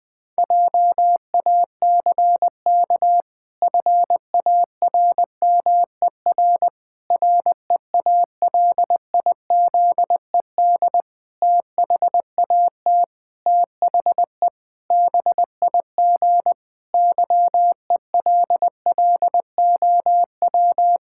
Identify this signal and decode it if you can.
Morse code, heard as a keyed tone, about 20 wpm, 700 Hz. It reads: JACK FARMER REALIZED THAT THE BIG YELLOW
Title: Pangram Challenge v2 - 20wpm